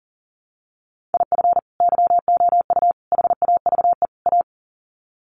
Morse code, heard as a keyed tone, about 40 wpm, 700 Hz. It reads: IF YOU HAVE A
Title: Sets of 4 Words 40wpm